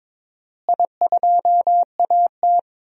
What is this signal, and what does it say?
Morse code, heard as a keyed tone, about 22 wpm, 700 Hz. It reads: I2AT
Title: DXCC-100 Call Signs - 4 Characters - Repeated 2x - 22wpm